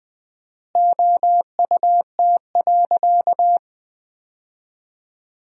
Morse code, heard as a keyed tone, about 20 wpm, 700 Hz. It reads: OUT.
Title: 4x Word-Spacing - Sentences from Top 200 Words - 20wpm